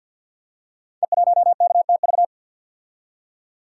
Morse code, heard as a keyed tone, about 52 wpm, 690 Hz. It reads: E1KTV
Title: Code Groups - 5 Letter:Numbers - 50wpm